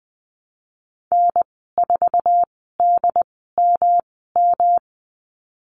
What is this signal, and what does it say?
Morse code, heard as a keyed tone, about 19 wpm, spaced as for 14 wpm, 700 Hz. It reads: N4DMM
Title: 122 Lesson 20g - Number 3 - Call Signs 20wpm